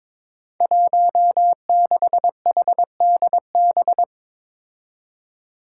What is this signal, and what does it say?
Morse code, heard as a keyed tone, about 22 wpm, 700 Hz. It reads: 16HDB
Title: Five Challenging Letter-Number Combinations 22wpm